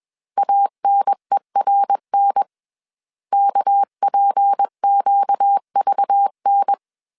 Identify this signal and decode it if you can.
Morse code, heard as a keyed tone, about 21 wpm, 800 Hz. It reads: ANERN KPQVN
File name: CW / DET / 1 6 1500C S A 20wpm 2048